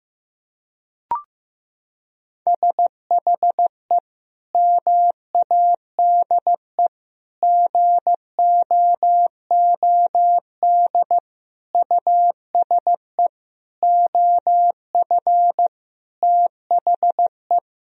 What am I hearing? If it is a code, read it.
Morse code, heard as a keyed tone, about 15 wpm, 700 Hz. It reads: SHE MADE GOOD USE OF THE
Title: Sentences from Top 1000 Words - Review Part 4 - 15wpm